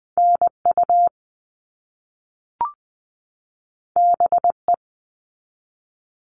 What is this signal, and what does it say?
Morse code, heard as a keyed tone, about 20 wpm, 700 Hz. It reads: NU BE
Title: All Cut Numbers - 2-Digit 20wpm